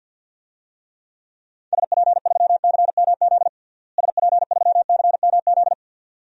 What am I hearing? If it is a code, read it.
Morse code, heard as a keyed tone, about 49 wpm, 700 Hz. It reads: SP3CMZ SP3CMZ
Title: DXCC-100 Call Signs - 6 Characters - Repeated 2x - 50wpm